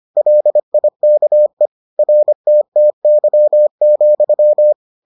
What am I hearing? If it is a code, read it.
Morse code, heard as a keyed tone, about 25 wpm, 600 Hz. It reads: LIKE RTTY,